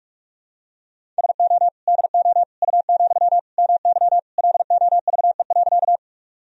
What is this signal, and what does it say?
Morse code, heard as a keyed tone, about 46 wpm, 700 Hz. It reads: SO DO U, MY LOVE.
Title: Sentences from Top 400 Words and Abbreviations 45wpm